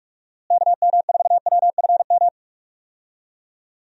Morse code, heard as a keyed tone, about 46 wpm, 700 Hz. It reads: KM4WFM
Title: USA Call Signs - 6 Characters - Repeated 3x - 45wpm